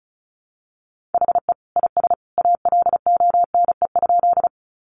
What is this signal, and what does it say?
Morse code, heard as a keyed tone, about 34 wpm, 700 Hz. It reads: HE IS ALONE?
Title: Sentences from Top 1000 Words 35wpm